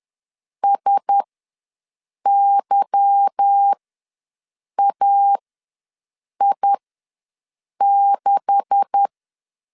Morse code, heard as a keyed tone, about 10 wpm, spaced as for 6 wpm, 800 Hz. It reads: SYAI6
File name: LEZIONI / 9 286 ws 10wpm 4032